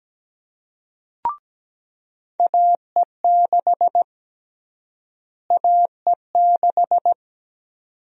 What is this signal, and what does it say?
Morse code, heard as a keyed tone, about 17 wpm, 700 Hz. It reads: AE6 AE6
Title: USA Call Sign Prefixes - Repeated 2x - 17wpm